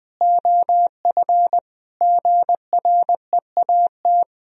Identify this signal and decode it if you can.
Morse code, heard as a keyed tone, about 20 wpm, 700 Hz. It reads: OF GREAT
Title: Sentences from Top 1000 Words - Review Part 2 - 20wpm